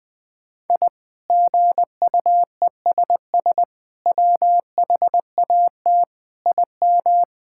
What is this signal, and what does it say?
Morse code, heard as a keyed tone, about 20 wpm, 700 Hz. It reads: I GUESS WHAT IM
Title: Sets of 8 Words - 20wpm